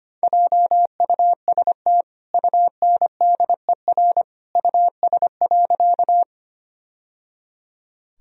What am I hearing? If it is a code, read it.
Morse code, heard as a keyed tone, about 25 wpm, 700 Hz. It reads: JUST UNDER US.
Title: Sentences from Top 1000 Words - Review Part 1 - 25wpm